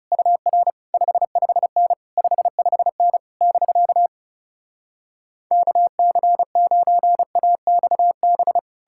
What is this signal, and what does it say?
Morse code, heard as a keyed tone, about 40 wpm, 700 Hz. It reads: UR 55N 55N <BK> KC9AXB